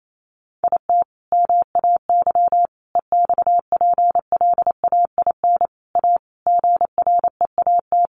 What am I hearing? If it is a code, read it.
Morse code, heard as a keyed tone, about 28 wpm, 700 Hz. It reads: IT MAY EXPLAIN A GREAT
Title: Sentences from Top 1100 Words 28wpm